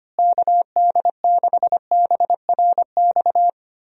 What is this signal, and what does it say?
Morse code, heard as a keyed tone, about 25 wpm, 700 Hz. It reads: KD6BRX
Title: USA Call Signs - 6 Characters - Repeated 3x - 25wpm